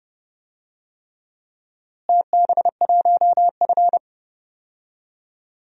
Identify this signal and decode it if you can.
Morse code, heard as a keyed tone, about 30 wpm, 700 Hz. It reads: TB1F
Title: DXCC-100 Call Signs - 4 Characters - Repeated 3x - 30wpm